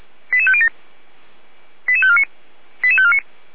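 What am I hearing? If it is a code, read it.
Five-tone selective call (ZVEI), heard as quick runs of five tones, tones of about 70 ms. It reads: 88507 88549 88549
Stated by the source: ZVEI five-tone selective call 88505 received by SDR on channel path 455.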